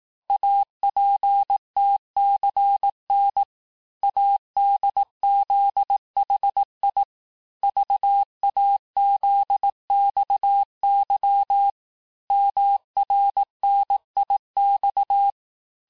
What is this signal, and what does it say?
Morse code, heard as a keyed tone, about 18 wpm, 800 Hz. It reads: APTCN ADZHI VAZXY MRNIX
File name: cours-du-lundi files / cours-1594--teste-chiffre-vitesse-17mpm